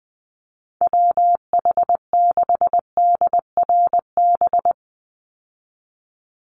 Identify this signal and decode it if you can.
Morse code, heard as a keyed tone, about 20 wpm, 700 Hz. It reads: WH6DRB
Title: USA Call Signs 20wpm